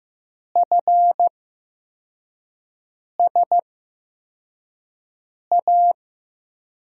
Morse code, heard as a keyed tone, about 15 wpm, spaced as for 4 wpm, 700 Hz. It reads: FSA
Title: USA Call Signs - 6 Characters - 8x Character Spacing - 15wpm